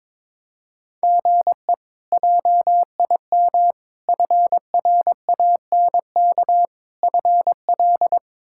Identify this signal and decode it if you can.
Morse code, heard as a keyed tone, about 22 wpm, 700 Hz. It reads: GE JIM FRANK FL
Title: K1USN SST QSO Practice - 22wpm